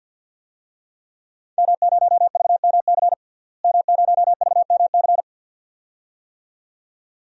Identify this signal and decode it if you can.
Morse code, heard as a keyed tone, about 53 wpm, 700 Hz. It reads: M0VMC M0VMC
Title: DXCC-100 Call Signs - 5 Characters - Repeated 2x - 50wpm